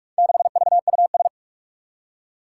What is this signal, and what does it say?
Morse code, heard as a keyed tone, about 43 wpm, 700 Hz. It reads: 6VUS
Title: Four Challenging Letter-Number Combinations 45wpm